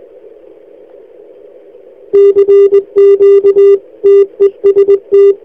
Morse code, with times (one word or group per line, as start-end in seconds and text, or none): 2.1-3.8 CQ
4.0-5.3 TEST